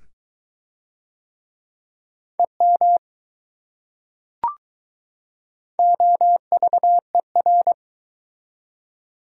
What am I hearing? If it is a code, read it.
Morse code, heard as a keyed tone, about 22 wpm, 700 Hz. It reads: EM OVER